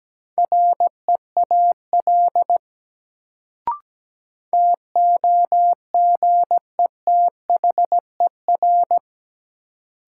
Morse code, heard as a keyed tone, about 17 wpm, 700 Hz. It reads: REAL TOGETHER
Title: Sentences from Top 300 Words - New Word Drill - 17wpm